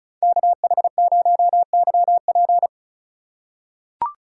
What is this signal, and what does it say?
Morse code, heard as a keyed tone, about 36 wpm, 700 Hz. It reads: KH0YP
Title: USA Call Signs - 3x Character Spacing - 35wpm